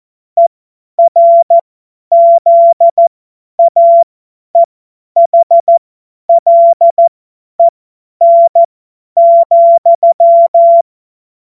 Morse code, heard as a keyed tone, about 14 wpm, spaced as for 10 wpm, 680 Hz. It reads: ERZAEHLEN,
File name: Morsekurs / Witze / Witz 20. Juli 2021, 50 F70